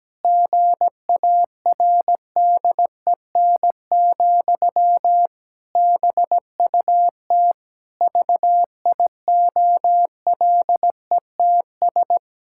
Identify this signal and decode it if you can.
Morse code, heard as a keyed tone, about 17 wpm, 700 Hz. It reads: GARDEN, BUT VIOLETS
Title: Pangram Challenge v2 - 17wpm